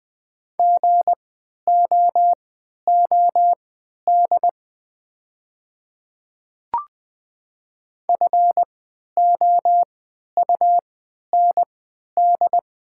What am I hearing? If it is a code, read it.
Morse code, heard as a keyed tone, about 20 wpm, spaced as for 11 wpm, 690 Hz. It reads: GOOD FOUND
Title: 3x Character Spacing - Top 300 Words - 20wpm